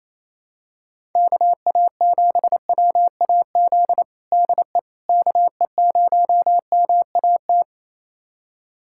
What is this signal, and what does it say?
Morse code, heard as a keyed tone, about 28 wpm, 700 Hz. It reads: KA7WAZ DE KE0MAT
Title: USA Callsign DE Callsign - 6 Characters - 28wpm